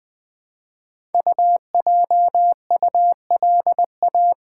Morse code, heard as a keyed tone, about 21 wpm, 700 Hz. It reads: UJULA
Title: Code Groups - 5 Letters - 20wpm